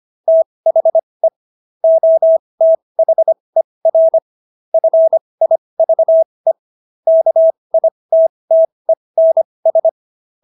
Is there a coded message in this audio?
Yes: Morse code from 0.3 to 9.9 s, heard as a keyed tone, about 25 wpm, spaced as for 20 wpm, 650 Hz. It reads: THE OTHER FIVE KITTENS